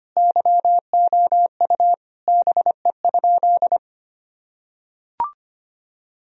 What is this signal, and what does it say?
Morse code, heard as a keyed tone, about 25 wpm, 700 Hz. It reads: YOU BE?
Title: Sentences from Top 1000 Words - Review Part 4 - 25wpm